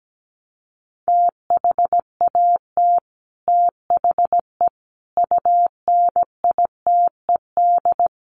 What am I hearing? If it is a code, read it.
Morse code, heard as a keyed tone, about 17 wpm, 700 Hz. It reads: THAT THE UNITED
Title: Sets of 5 Words Encore 17wpm